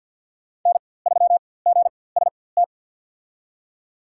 Morse code, heard as a keyed tone, about 48 wpm, spaced as for 23 wpm, 700 Hz. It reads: N3GST
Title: USA Call Signs - 5 Characters - 4x Character Spacing - 50wpm